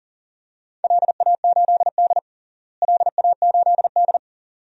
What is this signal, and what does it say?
Morse code, heard as a keyed tone, about 40 wpm, 690 Hz. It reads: LA8D LA8D
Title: DXCC-100 Call Signs - 4 Characters - Repeated 2x - 40wpm